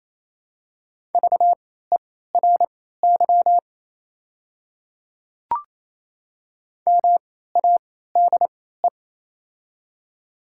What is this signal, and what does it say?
Morse code, heard as a keyed tone, about 28 wpm, spaced as for 16 wpm, 690 Hz. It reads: VERY MADE
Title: 3x Character Spacing - Top 300 Words - 28wpm